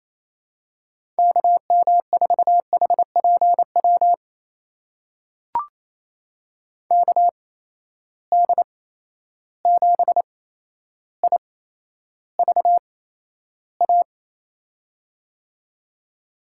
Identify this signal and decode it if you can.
Morse code, heard as a keyed tone, about 28 wpm, spaced as for 12 wpm, 700 Hz. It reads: KM4HPW KD7IVA
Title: USA Call Signs - 6 Characters - 8x Character Spacing - 28wpm